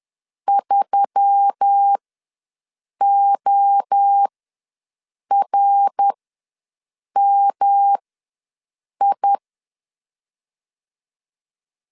Morse code, heard as a keyed tone, about 10 wpm, spaced as for 6 wpm, 800 Hz. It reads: 3ORMI
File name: LEZIONI / 7 GF9Z / 7 G ws 10wpm 3843